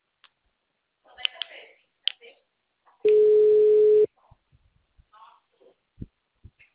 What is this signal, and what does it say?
Ringback tone (Europe, 425 Hz).
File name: out- -2014- - - .14